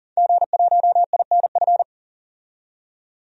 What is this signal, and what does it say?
Morse code, heard as a keyed tone, about 39 wpm, 700 Hz. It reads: G1INF
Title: DXCC-100 Call Signs - 5 Characters - 40wpm